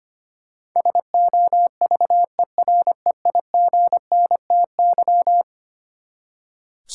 Morse code, heard as a keyed tone, about 25 wpm, 700 Hz. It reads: SOVEREIGNTY
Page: morse-code-ninja-practice-25wpm